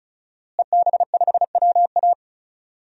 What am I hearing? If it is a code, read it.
Morse code, heard as a keyed tone, about 34 wpm, 700 Hz. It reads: EB5WA
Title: DXCC-100 Call Signs - 5 Characters - Repeated 3x - 35wpm